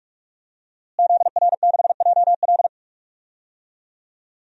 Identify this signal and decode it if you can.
Morse code, heard as a keyed tone, about 43 wpm, 700 Hz. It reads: ZR6JL
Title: DXCC-100 Call Signs - 5 Characters - Repeated 2x - 45wpm